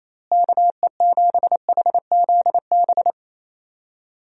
Morse code, heard as a keyed tone, about 28 wpm, 700 Hz. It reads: KE7HZB
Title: USA Call Signs - 6 Characters - Repeated 2x - 28wpm